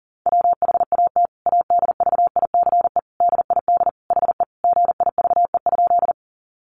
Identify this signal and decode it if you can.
Morse code, heard as a keyed tone, about 40 wpm, 700 Hz. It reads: WHAT ADVICE DID HE GIVE?
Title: Sentences from Top 1100 Words 40wpm